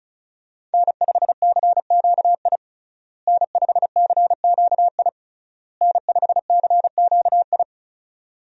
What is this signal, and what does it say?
Morse code, heard as a keyed tone, about 34 wpm, 700 Hz. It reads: N5CQI N5CQI N5CQI
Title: USA Call Signs - 5 Characters - Repeated 3x - 35wpm